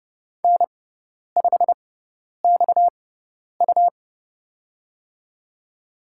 Morse code, heard as a keyed tone, about 30 wpm, spaced as for 10 wpm, 700 Hz. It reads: N5XU
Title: USA Call Signs - 4 Characters - 6x Character Spacing - 30wpm